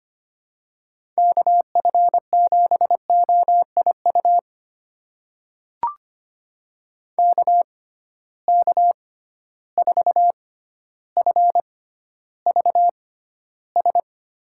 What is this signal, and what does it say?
Morse code, heard as a keyed tone, about 25 wpm, spaced as for 13 wpm, 700 Hz. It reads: KF7OIU KK4FVS
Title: USA Call Signs - 6 Characters - 6x Character Spacing - 25wpm